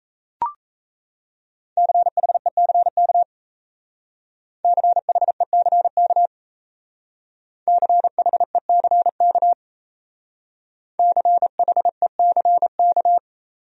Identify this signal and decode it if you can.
Morse code, heard as a keyed tone, about 33 wpm, 700 Hz. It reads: CHECK CHECK CHECK CHECK
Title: Speed-Racing Top 500 Words 28wpm